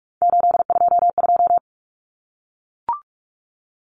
Morse code, heard as a keyed tone, about 44 wpm, 700 Hz. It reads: 822
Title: Common Cut Numbers - 3-Digit 45wpm